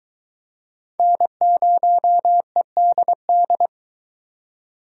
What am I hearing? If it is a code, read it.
Morse code, heard as a keyed tone, about 22 wpm, 700 Hz. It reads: N0EDD